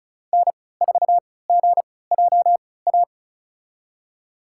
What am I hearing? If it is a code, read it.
Morse code, heard as a keyed tone, about 36 wpm, spaced as for 20 wpm, 700 Hz. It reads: N4GJA
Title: USA Call Signs - 5 Characters - 3x Character Spacing - 35wpm